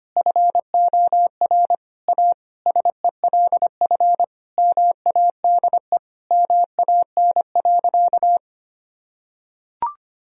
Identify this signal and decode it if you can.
Morse code, heard as a keyed tone, about 25 wpm, 700 Hz. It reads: FOR A SELF MADE MAN.
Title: Sentences from Top 1000 Words - Review Part 1 - 25wpm